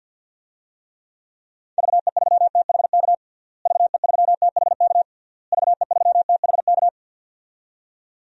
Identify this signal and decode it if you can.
Morse code, heard as a keyed tone, about 47 wpm, 700 Hz. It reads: VE3THK VE3THK VE3THK
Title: DXCC-100 Call Signs - 6 Characters - Repeated 3x - 50wpm